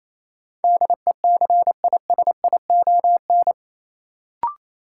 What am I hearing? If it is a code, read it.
Morse code, heard as a keyed tone, about 28 wpm, 700 Hz. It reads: DECISION
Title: Sentences from Top 1000 Words - New Word Drill - 28wpm